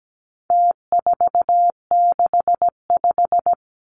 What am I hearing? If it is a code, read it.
Morse code, heard as a keyed tone, about 17 wpm, 700 Hz. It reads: T465
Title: Numbers - Contest Style - 4-Digit 17wpm